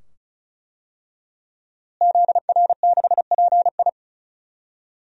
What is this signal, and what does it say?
Morse code, heard as a keyed tone, about 36 wpm, 700 Hz. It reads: ZR6PI
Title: DXCC-100 Call Signs - 5 Characters - Repeated 2x - 35wpm